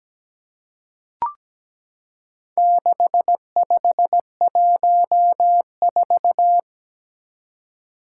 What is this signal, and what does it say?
Morse code, heard as a keyed tone, about 17 wpm, 700 Hz. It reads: 6514
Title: Four Challenging Letter-Number Combinations 17wpm